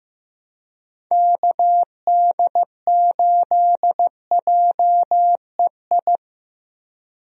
Morse code, heard as a keyed tone, about 15 wpm, 700 Hz. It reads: KD8JEI
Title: USA Call Signs - 6 Characters - Repeated 3x - 15wpm